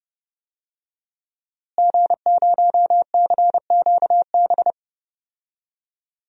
Morse code, heard as a keyed tone, about 30 wpm, 700 Hz. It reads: G0CQB
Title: DXCC-100 Call Signs - 5 Characters - Repeated 2x - 30wpm